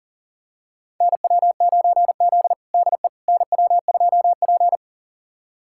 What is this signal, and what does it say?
Morse code, heard as a keyed tone, about 40 wpm, 700 Hz. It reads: NW9Z DE NW2P
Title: USA Callsign DE Callsign - 4 Characters - 40wpm